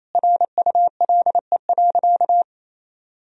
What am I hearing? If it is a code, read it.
Morse code, heard as a keyed tone, about 27 wpm, 700 Hz. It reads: RULE.
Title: Sentences from Top 1000 Words - Review Part 2 - 28wpm